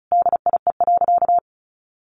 Morse code, heard as a keyed tone, about 34 wpm, 700 Hz. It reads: DIE.
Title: Sentences from Top 1300 Words 35wpm